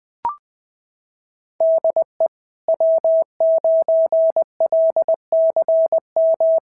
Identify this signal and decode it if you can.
Morse code, heard as a keyed tone, about 20 wpm, 660 Hz. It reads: DE W9LCM
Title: SOTA QSO Practice 20wpm